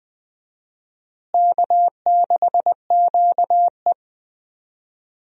Morse code, heard as a keyed tone, about 20 wpm, 700 Hz. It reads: K6QE
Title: USA Call Signs - 4 Characters - Repeated 3x - 20wpm